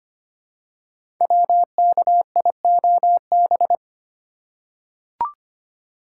Morse code, heard as a keyed tone, about 25 wpm, 700 Hz. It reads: WKIOB
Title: Code Groups - 5 Letters - 25wpm